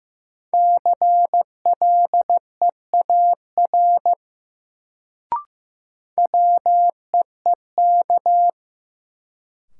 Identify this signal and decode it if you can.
Morse code, heard as a keyed tone, about 15 wpm, 700 Hz. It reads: CLEAR WEEK
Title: Sentences from Top 500 Words - New Word Drill - 15wpm